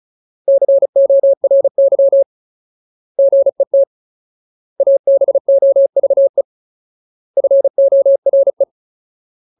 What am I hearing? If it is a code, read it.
Morse code, heard as a keyed tone, about 34 wpm, 550 Hz. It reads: CORY GET ABOVE FORE
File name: Ultimate Word List / 35WPM Ultimate